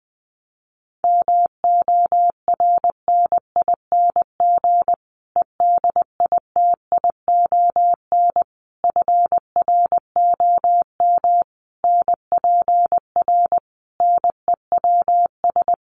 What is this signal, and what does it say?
Morse code, heard as a keyed tone, about 20 wpm, 700 Hz. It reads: MORNING EDITION FROM NPR NEWS
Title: Sets of 5 Words Encore 20wpm